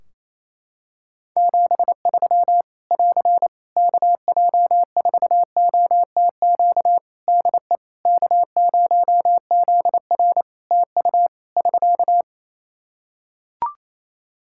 Morse code, heard as a keyed tone, about 28 wpm, 700 Hz. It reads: 73 <AR> KJ4OTQ DE K0ZR TU <SK>
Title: Practice QSO - Exchange 4 - Ending - 28wpm